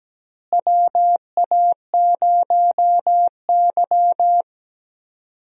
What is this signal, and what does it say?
Morse code, heard as a keyed tone, about 17 wpm, 700 Hz. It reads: WA0Y
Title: USA Call Signs - 4 Characters - Repeated 3x - 17wpm